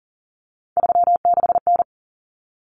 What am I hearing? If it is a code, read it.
Morse code, heard as a keyed tone, about 40 wpm, 700 Hz. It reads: 36N
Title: Common Cut Numbers - 3-Digit 40wpm